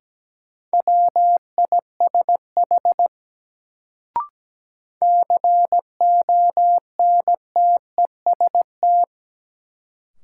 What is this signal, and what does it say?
Morse code, heard as a keyed tone, about 17 wpm, 700 Hz. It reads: WISH CONTEST
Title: Sentences from Top 1600 Words - New Word Drill Alt - 17wpm